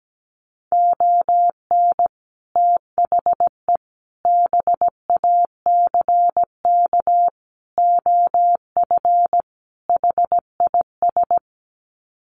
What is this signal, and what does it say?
Morse code, heard as a keyed tone, about 17 wpm, 700 Hz. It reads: ON THE BACK OF HIS
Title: Sets of 5 Words Encore 17wpm